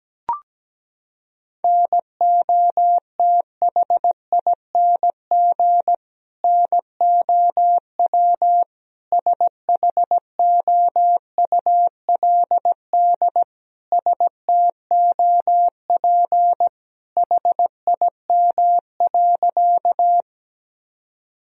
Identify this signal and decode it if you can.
Morse code, heard as a keyed tone, about 17 wpm, 700 Hz. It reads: NOTHING NOW SHOULD STOP HIM.
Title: Sentences from Top 1000 Words - Review Part 1 - 17wpm